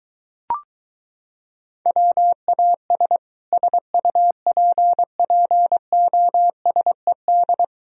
Morse code, heard as a keyed tone, about 23 wpm, 700 Hz. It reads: WAS SUPPOSED